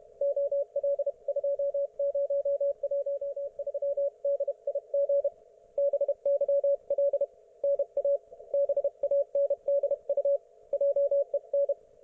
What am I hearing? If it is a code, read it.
Morse code, heard as a keyed tone, about 30 wpm, 560 Hz. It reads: OL2013DIG BYL NA BANDU JEN